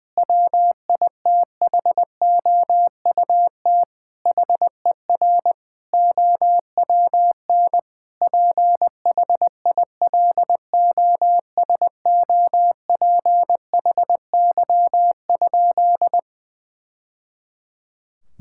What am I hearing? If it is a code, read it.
Morse code, heard as a keyed tone, about 20 wpm, 700 Hz. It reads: WITHOUT HER OWN PHILOSOPHY?
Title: Sentences from Top 1000 Words - Review Part 6 - 20wpm